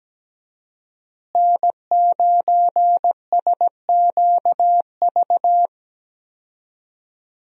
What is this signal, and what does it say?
Morse code, heard as a keyed tone, about 17 wpm, 700 Hz. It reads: N9SQV
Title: USA Call Signs - 5 Characters - Repeated 2x - 17wpm